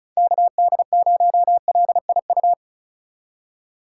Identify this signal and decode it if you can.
Morse code, heard as a keyed tone, about 42 wpm, 700 Hz. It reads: KD0LIU
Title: Speed Racing - USA 2x3 Call Signs 35wpm